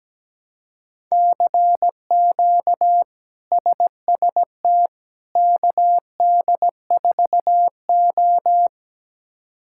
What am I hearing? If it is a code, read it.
Morse code, heard as a keyed tone, about 17 wpm, 700 Hz. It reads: CQ SST KD4O
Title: K1USN SST QSO Practice - 17wpm